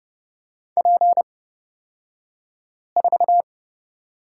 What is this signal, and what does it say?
Morse code, heard as a keyed tone, about 31 wpm, spaced as for 5 wpm, 690 Hz. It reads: P4
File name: Lesson 36d - BT - Two Characters 30wpm